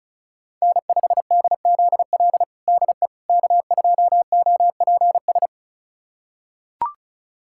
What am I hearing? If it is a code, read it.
Morse code, heard as a keyed tone, about 36 wpm, 690 Hz. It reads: N5DZL DE K2OPS
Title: USA Callsign DE Callsign - 5 Characters - 35wpm